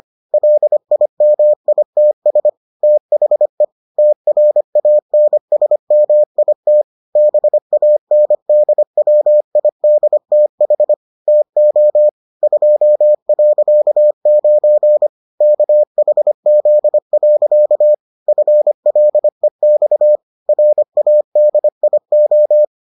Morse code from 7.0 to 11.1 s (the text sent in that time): BANDWIDTH